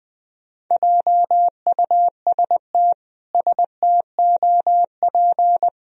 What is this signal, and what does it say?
Morse code, heard as a keyed tone, about 20 wpm, 700 Hz. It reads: JUST STOP
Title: Sentences from Top 1000 Words - Review Part 1 - 20wpm